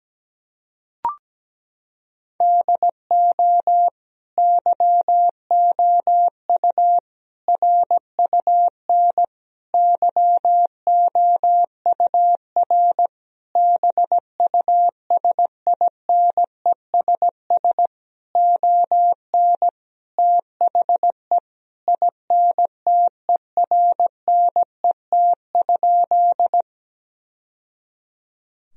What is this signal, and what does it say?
Morse code, heard as a keyed tone, about 17 wpm, 700 Hz. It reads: DO YOU RUN YOUR BUSINESS ON THE INTERNET?
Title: Sentences from Top 1000 Words - Review Part 4 - 17wpm